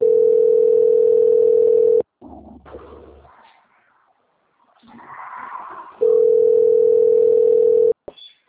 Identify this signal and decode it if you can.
Ringback tone (North America), 440 + 480 Hz.